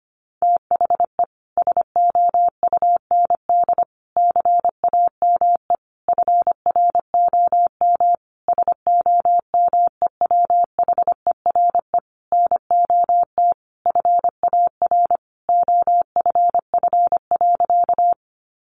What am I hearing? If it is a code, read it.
Morse code, heard as a keyed tone, about 25 wpm, 700 Hz. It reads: THE SOUND CAME FROM SOMEWHERE NOT FAR OFF.
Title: Sentences from Top 1400 Words 25wpm